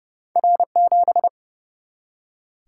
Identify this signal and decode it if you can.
Morse code, heard as a keyed tone, about 30 wpm, 700 Hz. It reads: R7
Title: Single-Letter Single-Number 30wpm